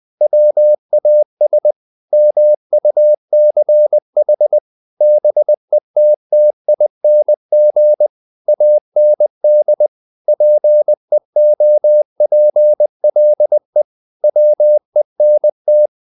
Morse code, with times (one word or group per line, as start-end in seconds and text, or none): 0.2-1.7 WAS
2.1-4.6 MUCH
5.0-8.1 BETTING
8.5-9.9 AND
10.3-13.8 PEOPLE
14.2-15.9 WENT